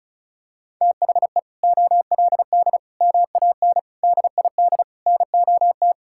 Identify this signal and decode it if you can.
Morse code, heard as a keyed tone, about 36 wpm, 700 Hz. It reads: THE OLD MAN DID NOT
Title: Sentences from Top 1600 Words 35wpm